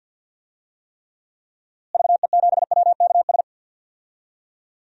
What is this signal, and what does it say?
Morse code, heard as a keyed tone, about 48 wpm, 700 Hz. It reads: VE7WKS
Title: DXCC-100 Call Signs - 6 Characters - Repeated 3x - 50wpm